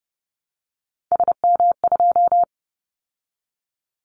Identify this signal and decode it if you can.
Morse code, heard as a keyed tone, about 30 wpm, 700 Hz. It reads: SM2
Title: Callsign Prefixes for Top 25 DXCC - 30wpm